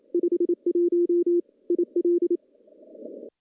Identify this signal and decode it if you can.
Morse code, heard as a keyed tone, about 27 wpm, 350 Hz. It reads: 51 IL
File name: contests / CW-SS-k5tr- -0100